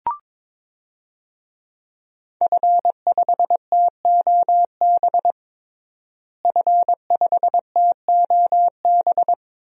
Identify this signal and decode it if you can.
Morse code, heard as a keyed tone, about 22 wpm, 700 Hz. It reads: F5TOB F5TOB
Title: DXCC-100 Call Signs - 5 Characters - Repeated 3x - 22wpm